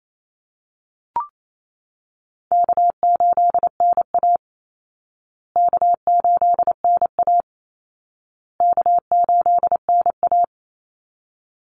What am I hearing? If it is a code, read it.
Morse code, heard as a keyed tone, about 28 wpm, 700 Hz. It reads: K8NA K8NA K8NA
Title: Easy Call Signs Repeated 3x - 28wpm